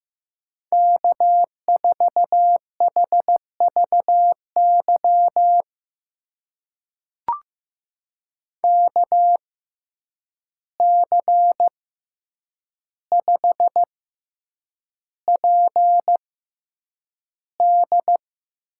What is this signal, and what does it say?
Morse code, heard as a keyed tone, about 15 wpm, spaced as for 8 wpm, 700 Hz. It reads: K4HVY KC5PD
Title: USA Call Signs - 6x Character Spacing - 15wpm